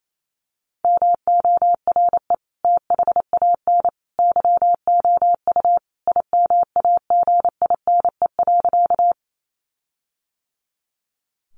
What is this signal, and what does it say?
Morse code, heard as a keyed tone, about 28 wpm, 700 Hz. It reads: MORE THAN YOU IMAGINE.
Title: Sentences from Top 1400 Words 28wpm